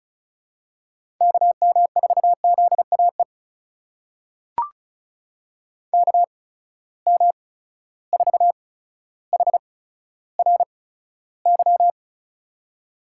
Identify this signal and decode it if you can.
Morse code, heard as a keyed tone, about 36 wpm, spaced as for 15 wpm, 700 Hz. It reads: KM4ZAE KM4HRY
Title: USA Call Signs - 6 Characters - 8x Character Spacing - 35wpm